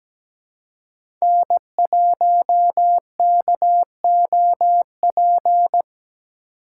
Morse code, heard as a keyed tone, about 17 wpm, 700 Hz. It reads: N1KOP
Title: USA Call Signs - 4x Character Spacing - 17wpm